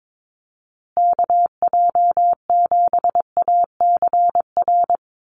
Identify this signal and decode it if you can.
Morse code, heard as a keyed tone, about 22 wpm, 700 Hz. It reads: KJ7ACR
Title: USA Call Signs Repeated 2x 22wpm